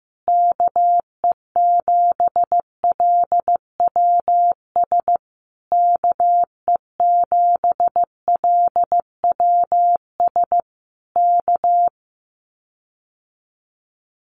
Morse code, heard as a keyed tone, about 15 wpm, 700 Hz. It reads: KE7LWS KE7LWS K
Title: Common QSO Phrases Encore 15wpm